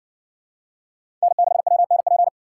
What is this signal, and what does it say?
Morse code, heard as a keyed tone, about 60 wpm, 700 Hz. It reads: N6WNP